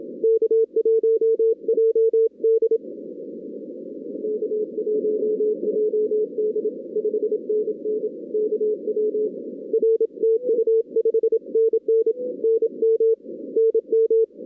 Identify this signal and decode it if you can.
Morse code, heard as a keyed tone, about 26 wpm, 440 Hz. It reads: K1JD K1JD 5NN KW R TU 5NN NM NM
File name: 2016 arrl dx cw / 2016-ARRL-DX-CW-K5TR- - 0-39